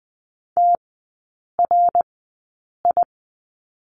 Morse code, heard as a keyed tone, about 20 wpm, spaced as for 8 wpm, 700 Hz. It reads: TRI
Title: 053 Lesson 10e - Letter r - Three Characters 20wpm